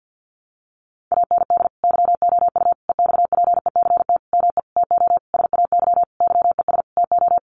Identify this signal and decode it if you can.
Morse code, heard as a keyed tone, about 50 wpm, 700 Hz. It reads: AND YOU EXPECT ME TO SAY YES TO
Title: Sentences from Top 1100 Words 50wpm